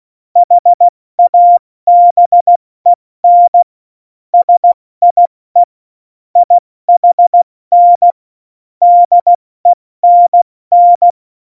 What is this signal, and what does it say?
Morse code, heard as a keyed tone, about 16 wpm, spaced as for 14 wpm, 700 Hz. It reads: HABEN SIE IHN DENN
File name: Morsekurs / Witze / Witz 04. September 2018, 70 F80